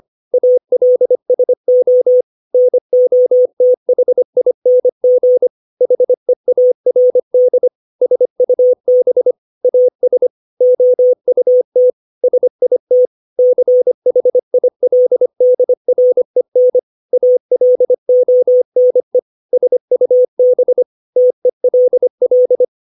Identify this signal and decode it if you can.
Morse code, heard as a keyed tone, about 25 wpm, 500 Hz. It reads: ALSO NOTHING HEARD SUB AS OUT SIT CHILDREN ALONE SUB TELL
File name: 632 words / 632 Words 25WPM